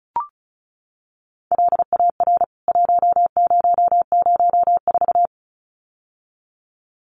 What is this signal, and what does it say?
Morse code, heard as a keyed tone, about 34 wpm, 700 Hz. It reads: LAR 1004
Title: CWT Exchange 35wpm